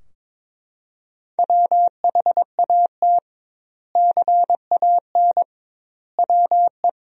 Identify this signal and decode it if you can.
Morse code, heard as a keyed tone, about 21 wpm, 700 Hz. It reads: WHAT CAN WE
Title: 2x Word-Spacing - Sentences from Top 100 Words - 22wpm